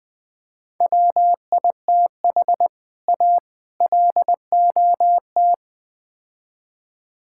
Morse code, heard as a keyed tone, about 20 wpm, 700 Hz. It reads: WITH A LOT
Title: Build A Phrase - 4-Word Phrases - 20wpm